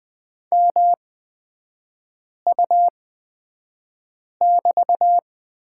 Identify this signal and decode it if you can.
Morse code, heard as a keyed tone, about 20 wpm, spaced as for 5 wpm, 700 Hz. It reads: MU<BT>
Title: 243 Lesson 39e - Letter z - Three Characters 20wpm